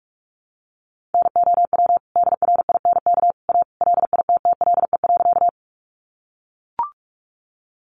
Morse code, heard as a keyed tone, about 44 wpm, 700 Hz. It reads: NOW DRINK A LITTLE.
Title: Sentences from Top 1500 Words 45wpm